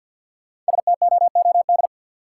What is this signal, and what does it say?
Morse code, heard as a keyed tone, about 45 wpm, 700 Hz. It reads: STOOD
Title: Sentences from Top 500 Words - New Word Drill - 50wpm